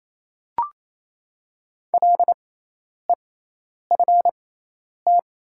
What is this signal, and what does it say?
Morse code, heard as a keyed tone, about 28 wpm, spaced as for 10 wpm, 700 Hz. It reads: LEFT
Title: 6x Character Spacing - Top 500 Words - 28wpm